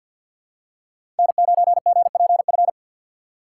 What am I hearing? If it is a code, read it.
Morse code, heard as a keyed tone, about 54 wpm, 700 Hz. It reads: N9GPF
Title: USA Call Signs - 5 Characters - 8x Character Spacing - 50wpm